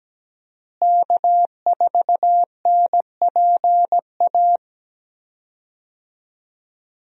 Morse code, heard as a keyed tone, about 17 wpm, 700 Hz. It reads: K4NPA
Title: USA Call Signs - 5 Characters - Repeated 3x - 17wpm